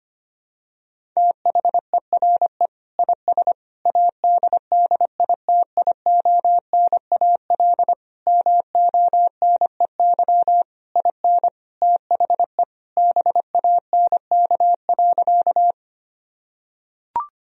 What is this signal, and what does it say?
Morse code, heard as a keyed tone, about 25 wpm, 700 Hz. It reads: THERE IS ADDITIONAL MONEY IN THE BANK.
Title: Sentences from Top 1000 Words - Review Part 4 - 25wpm